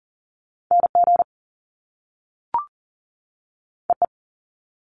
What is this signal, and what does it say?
Morse code, heard as a keyed tone, about 40 wpm, 700 Hz. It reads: NG EE
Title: Top Two Letter Combinations in Words 40wpm